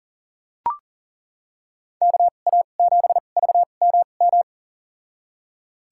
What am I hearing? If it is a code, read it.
Morse code, heard as a keyed tone, about 40 wpm, spaced as for 29 wpm, 700 Hz. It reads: KA7VMM
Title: USA Call Signs - 6 Characters - 2x Character Spacing - 40wpm